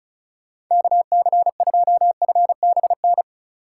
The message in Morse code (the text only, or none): KC2FBN